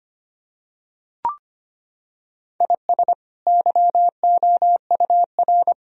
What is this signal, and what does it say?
Morse code, heard as a keyed tone, about 25 wpm, 700 Hz. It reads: IS YOUR
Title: Sentences from Top 1000 Words - Review Part 4 - 25wpm